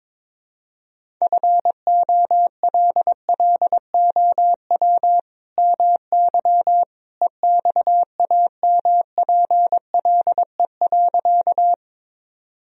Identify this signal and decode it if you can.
Morse code, heard as a keyed tone, about 22 wpm, 700 Hz. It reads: FOLLOW MY EXAMPLE.
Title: Sentences from Top 1000 Words - Review Part 1 - 22wpm